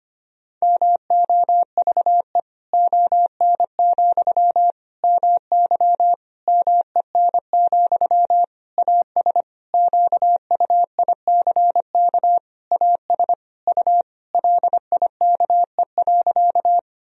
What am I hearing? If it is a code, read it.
Morse code, heard as a keyed tone, about 25 wpm, 700 Hz. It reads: MOVE ON, MY MEN, AS QUICK AS U LIKE.
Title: Sentences from Top 500 Words and Abbreviations 25wpm